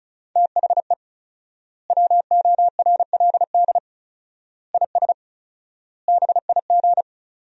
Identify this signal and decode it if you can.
Morse code, heard as a keyed tone, about 36 wpm, 700 Hz. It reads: THE WORLD IS BIG
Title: 4x Word-Spacing - Sentences from Top 400 Words - 35wpm